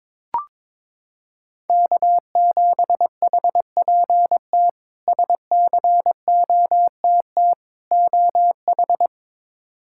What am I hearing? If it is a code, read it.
Morse code, heard as a keyed tone, about 22 wpm, 700 Hz. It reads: K7HPT SCOTT OH
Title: K1USN SST QSO Practice - 22wpm